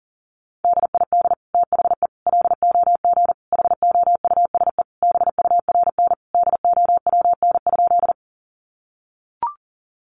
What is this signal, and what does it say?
Morse code, heard as a keyed tone, about 40 wpm, 700 Hz. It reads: DID THE LOG HOUSE BURN DOWN?
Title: Sentences from Top 1400 Words 40wpm